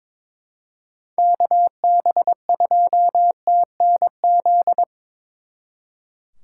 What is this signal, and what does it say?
Morse code, heard as a keyed tone, about 25 wpm, 700 Hz. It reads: KB2TNZ
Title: Speed Racing - USA 2x3 Call Signs 22wpm